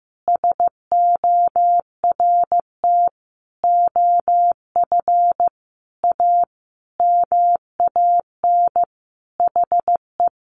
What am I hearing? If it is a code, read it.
Morse code, heard as a keyed tone, about 15 wpm, 700 Hz. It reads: SORT OF A MAN HE
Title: Sentences from Top 1300 Words 15wpm